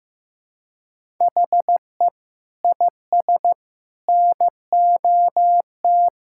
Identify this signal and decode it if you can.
Morse code, heard as a keyed tone, about 16 wpm, 700 Hz. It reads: HE IS NOT
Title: Sentences from Top 1000 Words - Review Part 5 - 15wpm